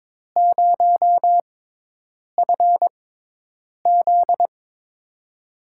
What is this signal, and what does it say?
Morse code, heard as a keyed tone, about 22 wpm, spaced as for 8 wpm, 700 Hz. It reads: 0FZ
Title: USA Call Signs - 4 Characters - 6x Character Spacing - 22wpm